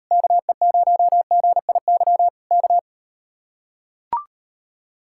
Morse code, heard as a keyed tone, about 38 wpm, 700 Hz. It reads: KE0GIY K